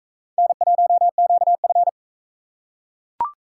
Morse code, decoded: N1QF

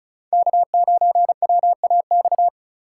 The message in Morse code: K9WAX

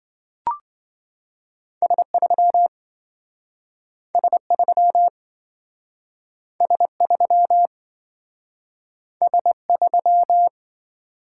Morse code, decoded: S3 S3 S3 S3

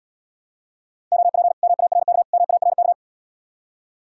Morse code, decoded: OP MARK MARK

